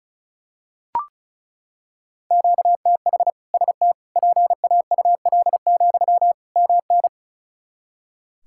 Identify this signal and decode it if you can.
Morse code, heard as a keyed tone, about 36 wpm, 700 Hz. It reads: QTH ST PAUL, MN
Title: K1USN SST QSO Practice 35wpm